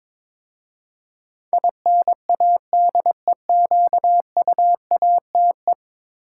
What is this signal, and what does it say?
Morse code, heard as a keyed tone, about 22 wpm, 700 Hz. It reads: INADEQUATE
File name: practice / 22wpm / 10 Letter Words - First 3 Letters Given - 22wpm